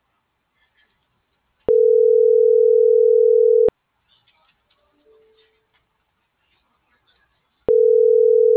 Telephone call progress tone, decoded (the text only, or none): ringback tone